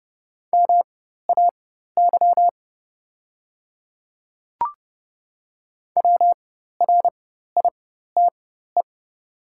Morse code, heard as a keyed tone, about 30 wpm, spaced as for 14 wpm, 700 Hz. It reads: MAY WRITE
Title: 4x Character Spacing - Top 100 Words - 30wpm